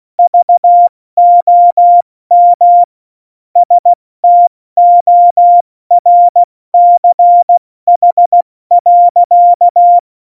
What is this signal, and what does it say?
Morse code, heard as a keyed tone, about 16 wpm, 700 Hz. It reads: VOM STORCH.
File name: Morsekurs / Witze / Witz 19. Februar 2019, 70 F80